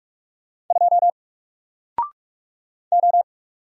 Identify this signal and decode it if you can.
Morse code, heard as a keyed tone, about 44 wpm, 700 Hz. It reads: 2 O
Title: Single Character (Letter:Number:Punctuation:Prosign) 45wpm